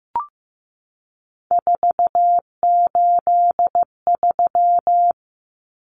Morse code, heard as a keyed tone, about 15 wpm, 700 Hz. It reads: 483
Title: Numbers - 3-Digit 15wpm